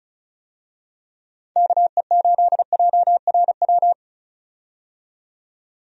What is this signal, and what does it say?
Morse code, heard as a keyed tone, about 36 wpm, 700 Hz. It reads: KE8JRW
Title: USA Call Signs - 6 Characters - 35wpm